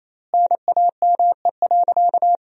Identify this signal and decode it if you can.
Morse code, heard as a keyed tone, about 28 wpm, 700 Hz. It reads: NAME.
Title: 3x Word-Spacing - Sentences from Top 200 Words - 28wpm